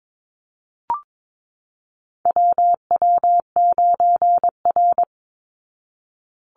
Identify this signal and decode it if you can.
Morse code, heard as a keyed tone, about 22 wpm, 700 Hz. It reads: WW9R
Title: Call Signs Repeated 3x - 22wpm